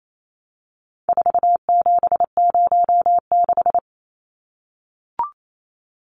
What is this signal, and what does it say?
Morse code, heard as a keyed tone, about 28 wpm, 700 Hz. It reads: 4706
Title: Numbers - 4-Digit 28wpm